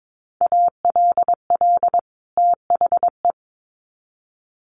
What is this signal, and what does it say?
Morse code, heard as a keyed tone, about 22 wpm, 700 Hz. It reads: ALL THE
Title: Sets of 2 Words 22wpm